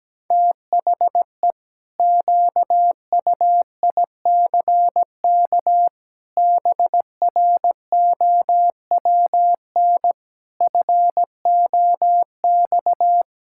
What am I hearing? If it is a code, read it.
Morse code, heard as a keyed tone, about 17 wpm, 700 Hz. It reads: THE QUICK BROWN FOX
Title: Pangram Challenge v2 - 17wpm